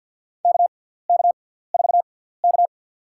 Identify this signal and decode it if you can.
Morse code, heard as a keyed tone, about 47 wpm, spaced as for 17 wpm, 700 Hz. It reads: KK4K
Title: USA Call Signs - 4 Characters - 6x Character Spacing - 50wpm